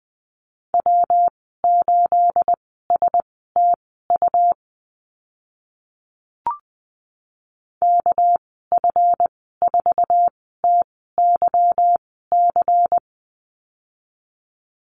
Morse code, heard as a keyed tone, about 20 wpm, spaced as for 14 wpm, 700 Hz. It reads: W8STU KF4TYC
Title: 245 Lesson 39g - Letter z - Call Signs 20wpm